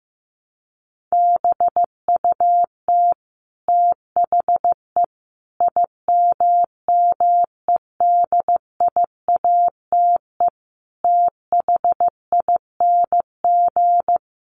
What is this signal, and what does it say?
Morse code, heard as a keyed tone, about 15 wpm, 700 Hz. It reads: BUT THE IMMEDIATE THING
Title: A Sound Of Thunder 15wpm